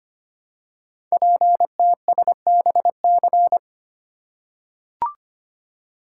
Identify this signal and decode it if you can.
Morse code, heard as a keyed tone, about 25 wpm, 700 Hz. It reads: PTSBC
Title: Code Groups - 5 Letters - 25wpm